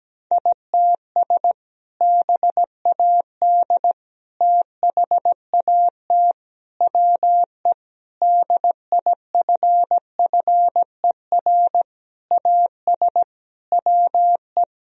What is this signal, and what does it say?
Morse code, heard as a keyed tone, about 17 wpm, 700 Hz. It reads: ITS BAD THAT WE DIFFER AS WE
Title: Sentences from Top 1000 Words - Review Part 1 - 17wpm